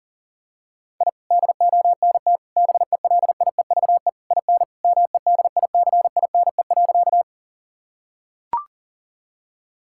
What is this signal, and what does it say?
Morse code, heard as a keyed tone, about 40 wpm, 700 Hz. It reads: I DONT BELIEVE IN MEDICINE.
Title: Sentences from Top 1000 Words - Review Part 6 - 40wpm